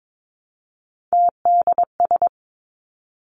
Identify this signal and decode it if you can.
Morse code, heard as a keyed tone, about 22 wpm, 700 Hz. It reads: TDS
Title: Three Challenging Letter-Number Combinations 22wpm